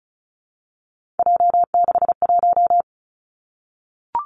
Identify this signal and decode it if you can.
Morse code, heard as a keyed tone, about 36 wpm, 700 Hz. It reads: J61
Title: Three Challenging Letter-Number Combinations 35wpm